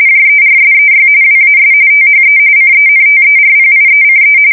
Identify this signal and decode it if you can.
RTTY, about 45.45 baud. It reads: JERSEY WAS THE CENTERPIECE
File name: fsk170 / battleshipnj